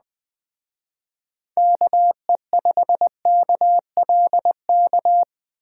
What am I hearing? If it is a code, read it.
Morse code, heard as a keyed tone, about 21 wpm, 700 Hz. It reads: KE5KLK
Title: Speed Racing - USA 2x3 Call Signs 17wpm